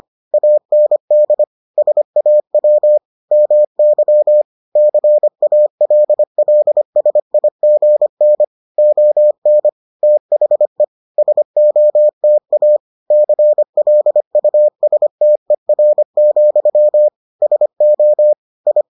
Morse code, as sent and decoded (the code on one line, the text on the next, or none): .- -. -.. / ... .- .-- / -- -.-- / -.-. .- .-.. .-.. ... .. --. -. / --- -. / - .... / ... --- - .- / -.-. .-.. ..- ... - . .-. --..-- / ... --- / ..
AND SAW MY CALLSIGN ON THE SOTA CLUSTER, SO I